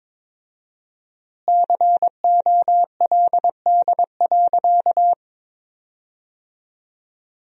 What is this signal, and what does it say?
Morse code, heard as a keyed tone, about 21 wpm, 700 Hz. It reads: COLD.
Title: 4x Word-Spacing - Sentences from Top 400 Words - 22wpm